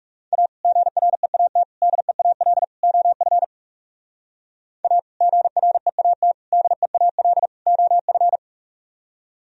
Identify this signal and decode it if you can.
Morse code, heard as a keyed tone, about 44 wpm, 700 Hz. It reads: A GREAT DEAL OF A GREAT DEAL OF